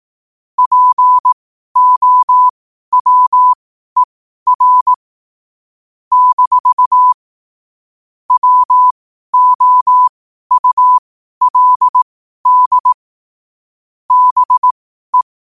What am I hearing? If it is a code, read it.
Morse code, heard as a keyed tone, about 18 wpm, spaced as for 12 wpm, 1000 Hz. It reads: POWER - WOULD BE
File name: qotdfiles / 12wpm